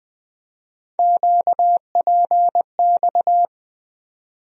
Morse code, heard as a keyed tone, about 20 wpm, 700 Hz. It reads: QPX
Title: USA Call Sign Postfixes - 3 Letters 20wpm